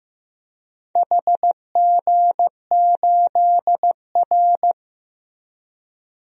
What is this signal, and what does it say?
Morse code, heard as a keyed tone, about 15 wpm, 700 Hz. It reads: HG8R
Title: DXCC-100 Call Signs - 4 Characters - Repeated 2x - 15wpm